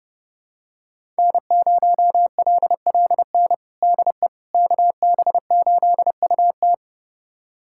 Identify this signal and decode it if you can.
Morse code, heard as a keyed tone, about 32 wpm, 700 Hz. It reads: N0LLN DE KB8UT
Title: USA Callsign DE Callsign - 5 Characters - 30wpm